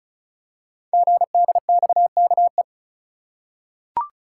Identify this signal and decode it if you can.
Morse code, heard as a keyed tone, about 36 wpm, 700 Hz. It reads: GDXKE
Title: Code Groups - 5 Letters - 35wpm